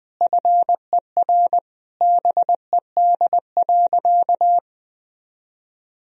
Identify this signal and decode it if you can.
Morse code, heard as a keyed tone, about 20 wpm, 700 Hz. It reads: FER BED.
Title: Sentences from Top 500 Words and Abbreviations 20wpm